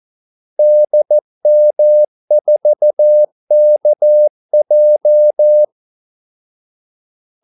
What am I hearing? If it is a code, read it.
Morse code, heard as a keyed tone, about 14 wpm, 600 Hz. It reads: DM4KJ
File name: Callsigns-14wpm